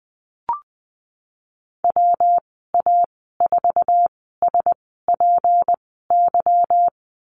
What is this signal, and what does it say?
Morse code, heard as a keyed tone, about 19 wpm, spaced as for 14 wpm, 700 Hz. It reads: WA4SPY
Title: 173 Lesson 28g - Number 9 - Call Signs 20wpm